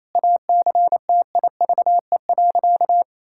Morse code, heard as a keyed tone, about 28 wpm, 700 Hz. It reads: ACTIVE.
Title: Sentences from Top 1000 Words - Review Part 6 - 28wpm